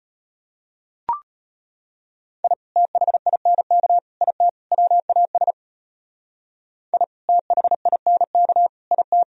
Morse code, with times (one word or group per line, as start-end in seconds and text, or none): 2.4-2.5 I
2.8-4.0 THINK
4.2-4.5 IT
4.7-5.5 WAS
6.9-7.0 I
7.3-8.7 THINK
8.9-9.2 IT